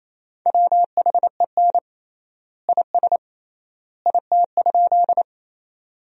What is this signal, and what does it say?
Morse code, heard as a keyed tone, about 28 wpm, 700 Hz. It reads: WHEN IS IT?
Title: 3x Word-Spacing - Sentences from Top 100 Words - 28wpm